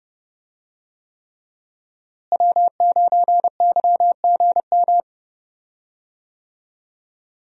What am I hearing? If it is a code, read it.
Morse code, heard as a keyed tone, about 30 wpm, 700 Hz. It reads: W9YGM
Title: USA Call Signs - 5 Characters - Repeated 3x - 30wpm